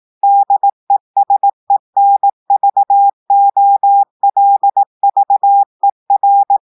Morse code, heard as a keyed tone, about 18 wpm, 800 Hz. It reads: DESENVOLVER
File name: audios / portugais / portugais texte en clair 2 vitesse 18 mots minutes